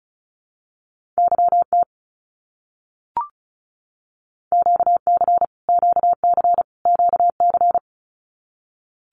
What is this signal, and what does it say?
Morse code, heard as a keyed tone, about 34 wpm, 700 Hz. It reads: YT QC QC QC
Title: Canadian Provinces and Territories - Repeated 3x - 35wpm